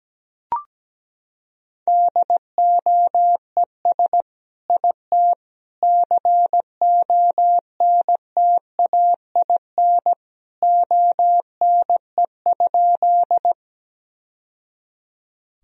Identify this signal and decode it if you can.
Morse code, heard as a keyed tone, about 17 wpm, 700 Hz. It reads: DOES IT CONTAIN ONE?
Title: Sentences from Top 1000 Words - Review Part 2 - 17wpm